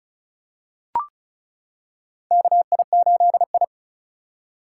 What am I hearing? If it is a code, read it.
Morse code, heard as a keyed tone, about 36 wpm, 700 Hz. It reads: KI8I
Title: K1USN SST QSO Practice - 35wpm